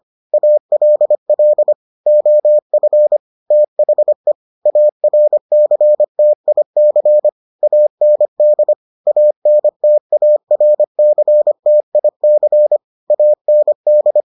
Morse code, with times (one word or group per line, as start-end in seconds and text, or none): 0.3-1.7 ALL
2.1-3.2 OF
3.5-4.3 THE
4.7-7.3 ARCTIC
7.6-8.7 AND
9.1-12.8 ANTARCTIC
13.1-14.2 AND